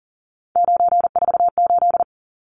Morse code, heard as a keyed tone, about 40 wpm, 700 Hz. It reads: 948
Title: Numbers - 3-Digit 40wpm